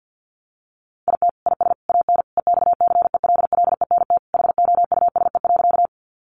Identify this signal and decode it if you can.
Morse code, heard as a keyed tone, about 50 wpm, 700 Hz. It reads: IT IS AN EXCELLENT HOUSE.
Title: Sentences from Top 1000 Words 50wpm